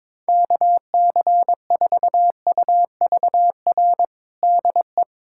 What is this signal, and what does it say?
Morse code, heard as a keyed tone, about 22 wpm, 700 Hz. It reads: KC4UVR DE
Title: USA Callsign DE Callsign 22wpm